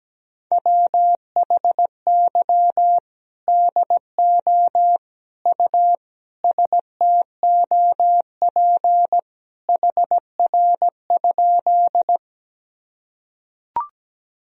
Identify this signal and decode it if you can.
Morse code, heard as a keyed tone, about 17 wpm, 700 Hz. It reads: WHY DO U STOP HR?
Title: Sentences from Top 300 Words and Abbreviations 17wpm